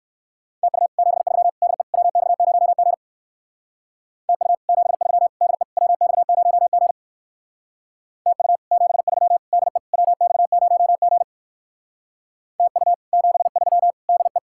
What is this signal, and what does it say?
Morse code, heard as a keyed tone, about 58 wpm, 700 Hz. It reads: TU 73 DE WX0G TU 73 DE WX0G TU 73 DE WX0G TU 73 DE